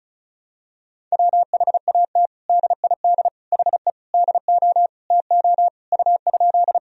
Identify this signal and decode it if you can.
Morse code, heard as a keyed tone, about 34 wpm, 700 Hz. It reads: WHAT DID HE DO TO U?
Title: Sentences from Top 100 Words and Abbreviations 35wpm